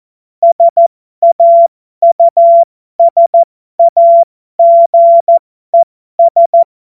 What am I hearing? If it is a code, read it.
Morse code, heard as a keyed tone, about 14 wpm, spaced as for 12 wpm, 680 Hz. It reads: SAUSAGES
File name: Morsekurs / Witze